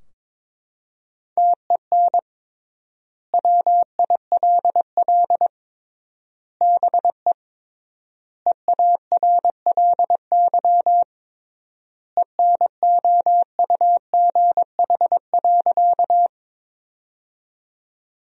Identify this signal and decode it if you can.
Morse code, heard as a keyed tone, about 22 wpm, 700 Hz. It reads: TEN WILL BE EARLY ENOUGH.
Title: 3x Word-Spacing - Sentences from Top 400 Words - 22wpm